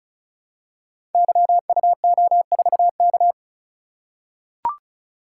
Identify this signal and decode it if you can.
Morse code, heard as a keyed tone, about 36 wpm, 700 Hz. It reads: YUO4K
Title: Code Groups - 5 Letter:Numbers - 35wpm